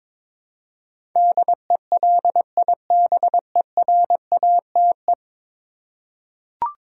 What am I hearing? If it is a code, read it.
Morse code, heard as a keyed tone, about 22 wpm, 700 Hz. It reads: DELIBERATE
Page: morse-code-ninja-practice-22wpm